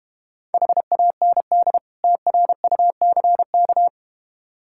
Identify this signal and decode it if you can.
Morse code, heard as a keyed tone, about 32 wpm, 700 Hz. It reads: HAND TRUCK